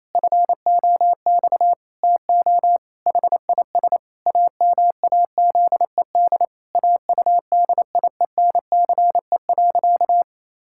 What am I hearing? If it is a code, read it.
Morse code, heard as a keyed tone, about 28 wpm, 700 Hz. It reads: FOX TO HIS AMAZED AUDIENCE.
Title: Pangram Challenge v2 - 28wpm